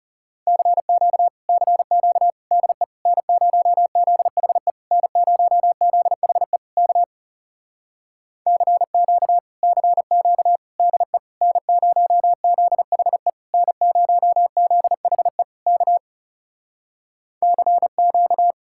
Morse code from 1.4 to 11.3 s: CQ DE N0ZHE N0ZHE K CQ CQ DE

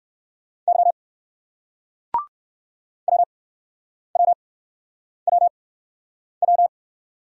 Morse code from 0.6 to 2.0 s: Y

